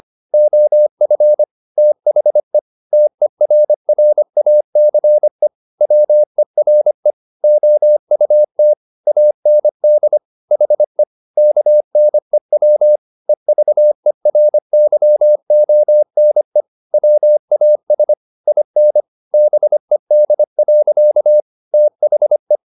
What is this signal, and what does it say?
Morse code, heard as a keyed tone, about 25 wpm, 600 Hz. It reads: OF THE TERRACE WERE OUT AND HE KNEW EVERYONE WAS IN BED. THE